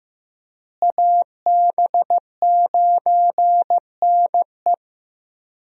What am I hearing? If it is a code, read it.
Morse code, heard as a keyed tone, about 16 wpm, 700 Hz. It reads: AB9NE
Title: USA Call Signs - 5 Characters - Repeated 3x - 15wpm